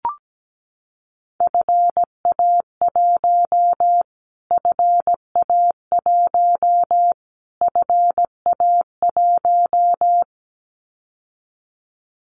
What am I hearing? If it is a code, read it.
Morse code, heard as a keyed tone, about 17 wpm, 700 Hz. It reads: FA1 FA1 FA1
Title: Callsign Prefixes for Top 25 DXCC - Repeated 3x - 17wpm